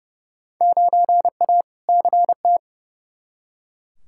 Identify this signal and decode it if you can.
Morse code, heard as a keyed tone, about 30 wpm, 700 Hz. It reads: 9A CT
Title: Field Day Exchanges Encore - 30wpm